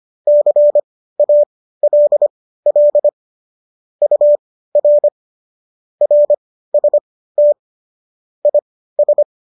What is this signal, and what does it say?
Morse code, heard as a keyed tone, about 25 wpm, spaced as for 15 wpm, 590 Hz. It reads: CALL UR RST IS
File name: qso208 15